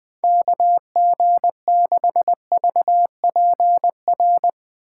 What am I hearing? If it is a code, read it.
Morse code, heard as a keyed tone, about 21 wpm, 700 Hz. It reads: KG6VPR
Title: Speed Racing - USA 2x3 Call Signs 17wpm